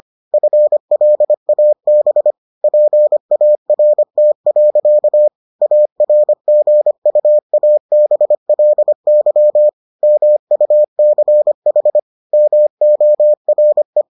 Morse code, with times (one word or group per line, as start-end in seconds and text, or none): 0.3-2.3 FLAB
2.6-5.3 PART.
5.6-9.7 ARGUABLY
10.0-12.0 MUCH
12.3-14.0 MORE